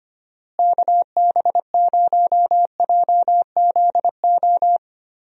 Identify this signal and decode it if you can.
Morse code, heard as a keyed tone, about 25 wpm, 700 Hz. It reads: KB0JZO
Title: USA Call Signs - 6 Characters - Repeated 2x - 25wpm